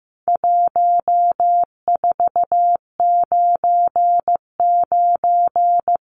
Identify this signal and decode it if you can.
Morse code, heard as a keyed tone, about 15 wpm, 700 Hz. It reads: 1499
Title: Numbers - 4-Digit 15wpm